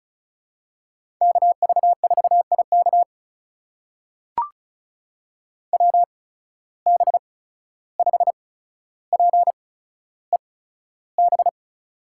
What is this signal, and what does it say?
Morse code, heard as a keyed tone, about 36 wpm, spaced as for 14 wpm, 700 Hz. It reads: KV4IK WB5PEB
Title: USA Call Signs - 8x Character Spacing - 35wpm